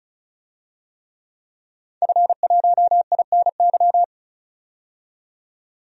Morse code, heard as a keyed tone, about 36 wpm, 700 Hz. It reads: F1INY
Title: DXCC-100 Call Signs - 5 Characters - 35wpm